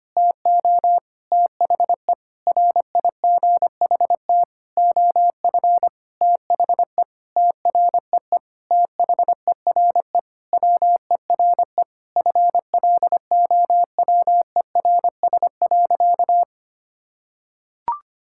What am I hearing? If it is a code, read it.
Morse code, heard as a keyed tone, about 25 wpm, 700 Hz. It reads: TO THE RIGHT OF THE TREE THERE WERE FLOWERS.
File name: practice / 25wpm / Sentences with 9-Word Phrases - 25wpm